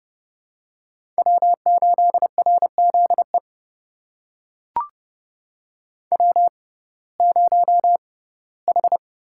Morse code, heard as a keyed tone, about 30 wpm, spaced as for 18 wpm, 700 Hz. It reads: W8RZE W0H
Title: USA Call Signs - 5 Characters - 6x Character Spacing - 30wpm